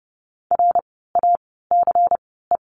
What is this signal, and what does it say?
Morse code, heard as a keyed tone, about 30 wpm, spaced as for 17 wpm, 700 Hz. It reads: RACE
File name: Lesson 16f - Letter c - Words 30wpm